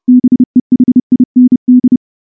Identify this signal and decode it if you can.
Morse code, heard as a keyed tone, about 30 wpm, 250 Hz. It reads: BEHIND
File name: morse / morse-training-30WPM